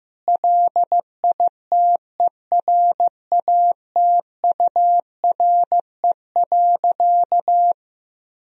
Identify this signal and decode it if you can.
Morse code, heard as a keyed tone, about 15 wpm, 700 Hz. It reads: LITERATURE.
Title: Sentences from Top 1000 Words - Review Part 3 - 15wpm